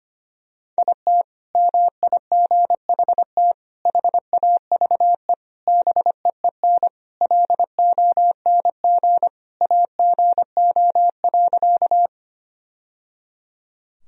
Morse code, heard as a keyed tone, about 25 wpm, 700 Hz. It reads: IT MIGHT HAVE BEEN LONG AGO.
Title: Sentences from Top 1000 Words - Review Part 3 - 25wpm